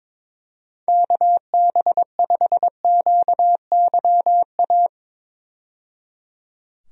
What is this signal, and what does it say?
Morse code, heard as a keyed tone, about 22 wpm, 700 Hz. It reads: KB5QYA
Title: USA Call Signs - 6 Characters - 22wpm